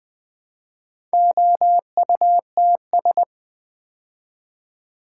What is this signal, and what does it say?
Morse code, heard as a keyed tone, about 20 wpm, 700 Hz. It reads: OUTS
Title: Build A Word - 2 Letters At A Time - 8-Letter Words - 20wpm